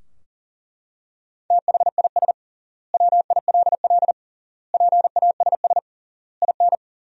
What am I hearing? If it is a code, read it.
Morse code, heard as a keyed tone, about 40 wpm, 700 Hz. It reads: THIS WILL PASS IN
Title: 3x Word-Spacing - Sentences from Top 400 Words - 40wpm